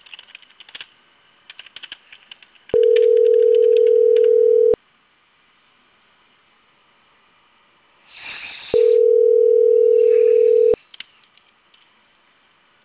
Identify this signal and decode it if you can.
Ringback tone (North America), 440 + 480 Hz.